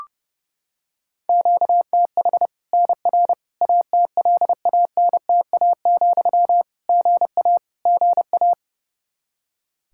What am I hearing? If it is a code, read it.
Morse code, heard as a keyed tone, about 30 wpm, 700 Hz. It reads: QTH NR ATLANTA, GA GA
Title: Practice QSO - Exchange 1 - Beginning - 30wpm